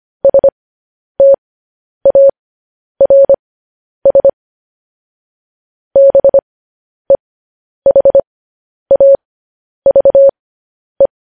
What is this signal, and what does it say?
Morse code, heard as a keyed tone, about 25 wpm, spaced as for 10 wpm, 550 Hz. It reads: STARS BEHAVE
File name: podcast / News headlines-1x-25x10